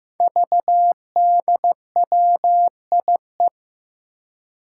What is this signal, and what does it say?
Morse code, heard as a keyed tone, about 15 wpm, 700 Hz. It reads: VDWIE
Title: Five Challenging Letter-Number Combinations 15wpm